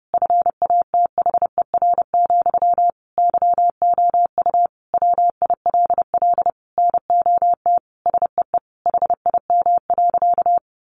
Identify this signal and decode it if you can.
Morse code, heard as a keyed tone, about 30 wpm, 700 Hz. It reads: FATHER, YOU WILL NOT SEE HIM.
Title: Sentences from Top 600 Words 30wpm